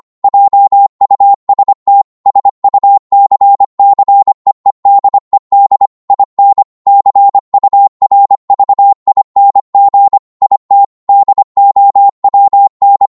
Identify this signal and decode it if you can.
Morse code, heard as a keyed tone, about 25 wpm, 800 Hz. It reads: JUST SUCCEEDED IN CURVING IT DOWN
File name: Morse / psg-25wpm